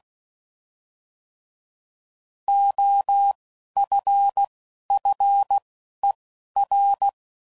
Morse code, heard as a keyed tone, about 16 wpm, spaced as for 11 wpm, 800 Hz. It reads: OFFER